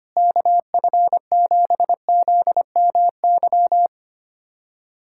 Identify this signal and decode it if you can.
Morse code, heard as a keyed tone, about 25 wpm, 700 Hz. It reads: KF7ZMY
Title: USA Callsign DE Callsign 25wpm